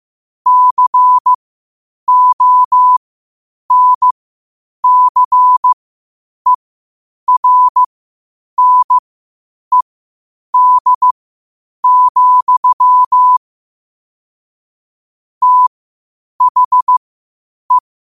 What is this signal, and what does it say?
Morse code, heard as a keyed tone, about 15 wpm, spaced as for 8 wpm, 1000 Hz. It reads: CONCERNED, THE
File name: qotdfiles / 8wpm